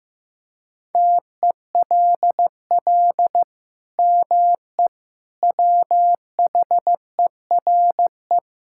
Morse code, heard as a keyed tone, about 15 wpm, 700 Hz. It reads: TELL ME WHERE
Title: Sentences from Top 1600 Words 15wpm